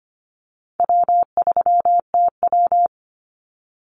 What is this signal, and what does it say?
Morse code, heard as a keyed tone, about 25 wpm, 700 Hz. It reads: W3TW
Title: Easy Call Signs - 25wpm